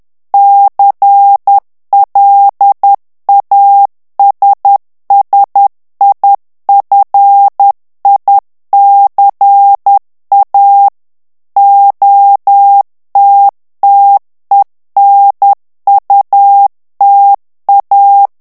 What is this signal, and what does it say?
Morse code, heard as a keyed tone, about 10 wpm, 800 Hz. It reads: CLASSIFICA OTTENUTA
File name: ITALIEN texte clair 2 vitesse 10 mots minute